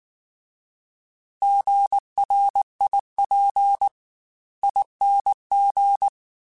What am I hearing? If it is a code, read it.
Morse code, heard as a keyed tone, about 19 wpm, 800 Hz. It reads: GRIP ING